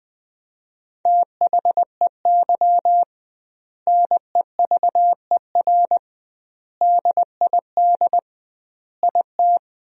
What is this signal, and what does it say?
Morse code, heard as a keyed tone, about 21 wpm, 700 Hz. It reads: THEY NEVER DID IT
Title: 2x Word-Spacing - Sentences from Top 300 Words - 20wpm